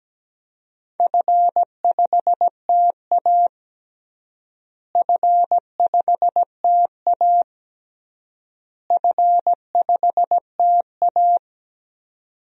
Morse code, heard as a keyed tone, about 17 wpm, 700 Hz. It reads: F5TA F5TA F5TA
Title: DXCC-100 Call Signs - 4 Characters - Repeated 3x - 17wpm